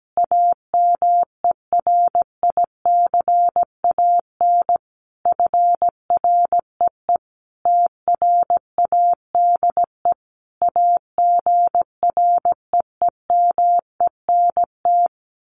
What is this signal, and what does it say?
Morse code, heard as a keyed tone, about 17 wpm, 700 Hz. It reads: AMERICAN FREE TRADE AGREEMENT
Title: Sets of 5 Words Encore 17wpm